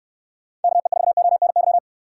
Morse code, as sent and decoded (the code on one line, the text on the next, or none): -.- . ....- --- -. -.--
KE4ONY